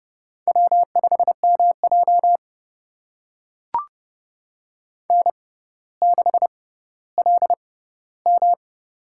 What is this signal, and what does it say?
Morse code, heard as a keyed tone, about 30 wpm, spaced as for 15 wpm, 700 Hz. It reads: W5MJ N6LM
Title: USA Call Signs - 4 Characters - 6x Character Spacing - 30wpm